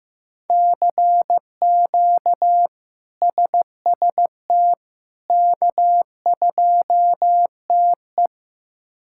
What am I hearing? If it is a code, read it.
Morse code, heard as a keyed tone, about 15 wpm, 700 Hz. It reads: CQ SST K2TE
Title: K1USN SST QSO Practice - 15wpm